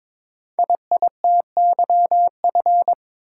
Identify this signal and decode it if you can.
Morse code, heard as a keyed tone, about 22 wpm, 700 Hz. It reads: IITYF
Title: Code Groups - 5 Letters - 22wpm